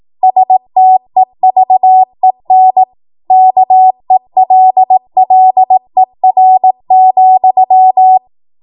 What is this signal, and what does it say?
Morse code, heard as a keyed tone, about 18 wpm, 750 Hz. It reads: STEVEN KELLER,
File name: Morse / 18WPM